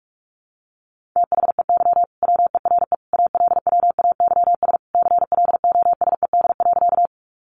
Morse code, heard as a keyed tone, about 44 wpm, 700 Hz. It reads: THEY WERE ALWAYS CLOSED.
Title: Sentences from Top 1400 Words 45wpm